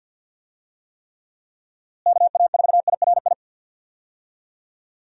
Morse code, heard as a keyed tone, about 50 wpm, 700 Hz. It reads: KA4IRI
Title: USA Call Signs - 6 Characters - Repeated 2x - 50wpm